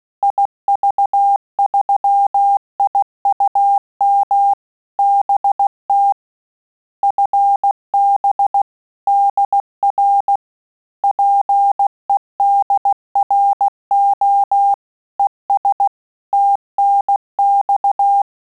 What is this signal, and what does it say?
Morse code, heard as a keyed tone, about 16 wpm, 800 Hz. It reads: IV3IUM BT FB DR PEDRO ES TNX
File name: QSO CW 75 CPM